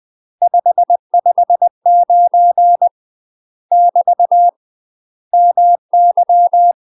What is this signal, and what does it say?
Morse code, heard as a keyed tone, about 20 wpm, 700 Hz. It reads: 559 <BT> MY
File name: UK MASTER QSOs TRAINING / 20WPM / UK MASTER QSOs 0022